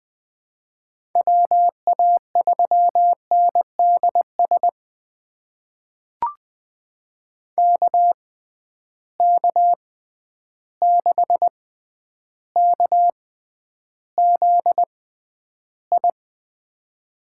Morse code, heard as a keyed tone, about 21 wpm, spaced as for 10 wpm, 700 Hz. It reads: WA3NDS KK6KZI
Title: USA Call Signs - 6x Character Spacing - 20wpm